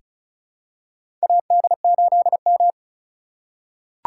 Morse code, heard as a keyed tone, about 35 wpm, 700 Hz. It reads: AD8M
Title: USA Call Signs - 4 Characters - 3x Character Spacing - 35wpm